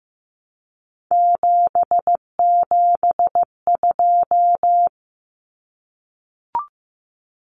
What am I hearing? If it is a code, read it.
Morse code, heard as a keyed tone, about 15 wpm, 700 Hz. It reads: 772
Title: Numbers - 3-Digit 15wpm